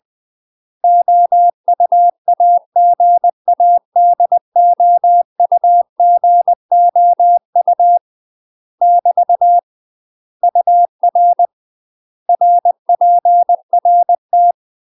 Morse code, heard as a keyed tone, about 20 wpm, 700 Hz. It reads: OUAGADOUGOU <BT> UR RPRT
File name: UK MASTER QSOs TRAINING / 20WPM / UK MASTER QSOs 0034